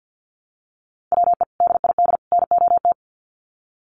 Morse code, heard as a keyed tone, about 50 wpm, 700 Hz. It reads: WE DID NOT
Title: Sets of 3 Words 50wpm